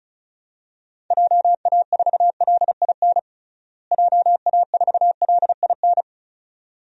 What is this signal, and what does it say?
Morse code, heard as a keyed tone, about 36 wpm, 700 Hz. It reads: JA4LIN JA4LIN
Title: DXCC-100 Call Signs - 6 Characters - Repeated 2x - 35wpm